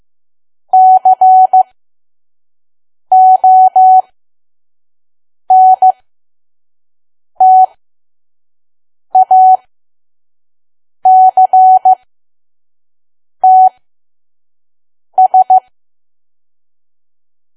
Morse code, heard as a keyed tone, about 15 wpm, spaced as for 5 wpm, 750 Hz. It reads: CONTACTS
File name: Archive / 5 WPM files / PM